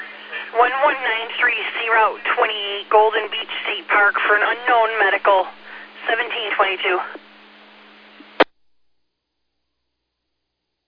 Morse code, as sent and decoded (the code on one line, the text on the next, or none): none